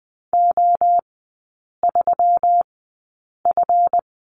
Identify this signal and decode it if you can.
Morse code, heard as a keyed tone, about 20 wpm, spaced as for 8 wpm, 700 Hz. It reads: O3F
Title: 152 Lesson 25e - Letter p - Three Characters 20wpm